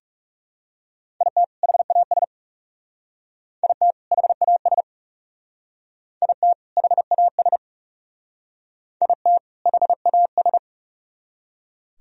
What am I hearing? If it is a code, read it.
Morse code, heard as a keyed tone, about 39 wpm, 700 Hz. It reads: IT HAS IT HAS IT HAS IT HAS